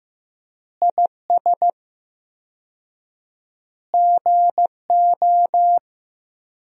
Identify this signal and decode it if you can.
Morse code, heard as a keyed tone, about 15 wpm, 700 Hz. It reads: IS GO
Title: 4x Word-Spacing - Sentences from Top 200 Words - 15wpm